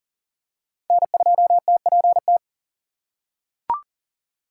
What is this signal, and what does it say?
Morse code, heard as a keyed tone, about 40 wpm, 700 Hz. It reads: N2TPT
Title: USA Call Signs - 5 Characters - 4x Character Spacing - 40wpm